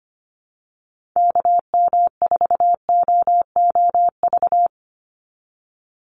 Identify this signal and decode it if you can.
Morse code, heard as a keyed tone, about 25 wpm, 700 Hz. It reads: KM4OOV
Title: USA Call Signs 25wpm